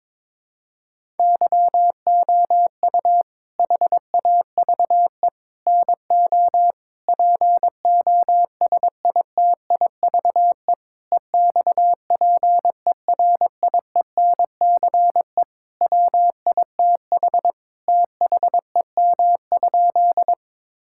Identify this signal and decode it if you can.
Morse code, heard as a keyed tone, about 22 wpm, 700 Hz. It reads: YOU HAVE NO POSITIVE EXPERIENCE WITH THEM?
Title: Sentences from Top 1000 Words - Review Part 5 - 22wpm